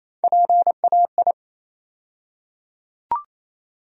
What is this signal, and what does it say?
Morse code, heard as a keyed tone, about 28 wpm, 700 Hz. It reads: PAI
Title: USA Call Sign Postfixes - 3 Letters 28wpm